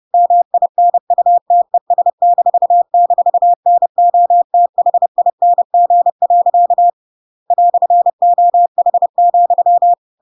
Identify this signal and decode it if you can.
Morse code, heard as a keyed tone, about 30 wpm, 700 Hz. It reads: MINUTES--NOTHING. "OH,
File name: Pinocchio / Pinocchio ch1 30 30 ik1hge